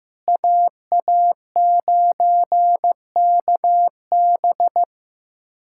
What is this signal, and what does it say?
Morse code, heard as a keyed tone, about 15 wpm, 700 Hz. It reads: AA9KB
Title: USA Callsign DE Callsign 15wpm